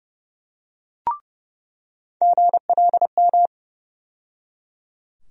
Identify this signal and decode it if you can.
Morse code, heard as a keyed tone, about 30 wpm, 700 Hz. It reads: GLM
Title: USA Call Sign Postfixes - 3 Letters 30wpm